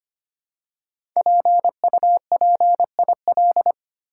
Morse code, heard as a keyed tone, about 25 wpm, 700 Hz. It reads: PUPIL
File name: 5 Letter Words - First 3 Letters Given - 25wpm